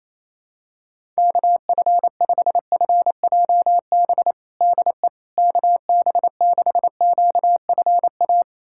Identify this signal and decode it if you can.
Morse code, heard as a keyed tone, about 28 wpm, 700 Hz. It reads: KF5FJB DE KB6QFA
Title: USA Callsign DE Callsign - 6 Characters - 28wpm